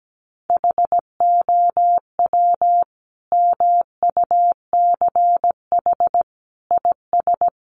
Morse code, heard as a keyed tone, about 17 wpm, 700 Hz. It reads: HOW MUCH IS
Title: Sentences from Top 1100 Words 17wpm